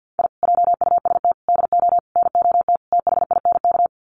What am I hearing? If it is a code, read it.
Morse code, heard as a keyed tone, about 50 wpm, 700 Hz. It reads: I JUST DO NOT THINK
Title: Sets of 5 Words Encore 50wpm